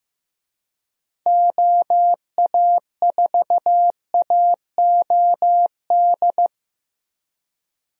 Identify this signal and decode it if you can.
Morse code, heard as a keyed tone, about 15 wpm, 700 Hz. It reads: OA4AOD
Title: USA Call Signs - 6 Characters - Repeated 2x - 15wpm